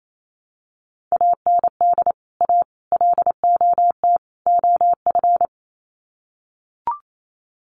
Morse code, heard as a keyed tone, about 28 wpm, 700 Hz. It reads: AND A LOT OF
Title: Sets of 4 Words 28wpm